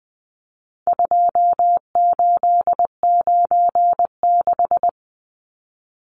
Morse code, heard as a keyed tone, about 20 wpm, 700 Hz. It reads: 2896
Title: Numbers - 4-Digit 20wpm